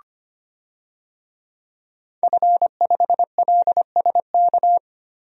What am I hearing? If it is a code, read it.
Morse code, heard as a keyed tone, about 25 wpm, 700 Hz. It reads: F5LSK
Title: DXCC-100 Call Signs - 5 Characters - 25wpm